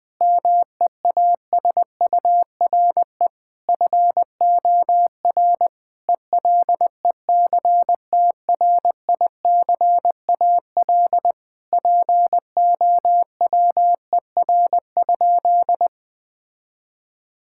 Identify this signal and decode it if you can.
Morse code, heard as a keyed tone, about 21 wpm, 700 Hz. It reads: MEASURE FOR ELECTRICAL POWER?
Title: Sentences from Top 1000 Words - Review Part 6 - 20wpm